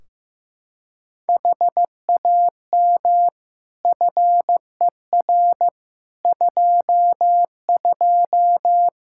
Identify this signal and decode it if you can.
Morse code, heard as a keyed tone, about 15 wpm, 700 Hz. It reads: HAM FER 22
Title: K1USN SST QSO Practice 15wpm